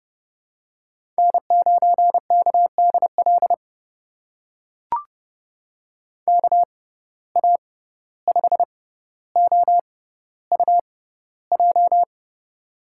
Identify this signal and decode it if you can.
Morse code, heard as a keyed tone, about 30 wpm, spaced as for 15 wpm, 700 Hz. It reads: N9KDL KA5OUJ
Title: USA Call Signs - 6x Character Spacing - 30wpm